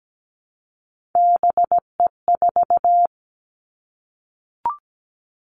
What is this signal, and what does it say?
Morse code, heard as a keyed tone, about 17 wpm, 700 Hz. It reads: BE4
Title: Three Challenging Letter-Number Combinations 17wpm